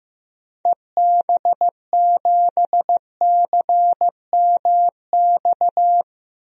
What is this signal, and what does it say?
Morse code, heard as a keyed tone, about 16 wpm, 700 Hz. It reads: EB7CMX
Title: DXCC-100 Call Signs - 6 Characters - 15wpm